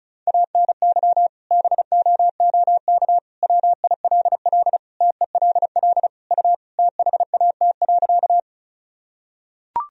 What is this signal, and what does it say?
Morse code, heard as a keyed tone, about 34 wpm, 700 Hz. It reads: ANY BOOK WILL TELL U THAT.
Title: Sentences from Top 300 Words and Abbreviations 35wpm